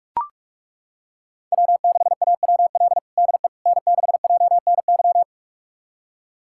Morse code, heard as a keyed tone, about 42 wpm, 700 Hz. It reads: W6AWL DE N6JNY
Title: USA Callsign DE Callsign - 5 Characters - 45wpm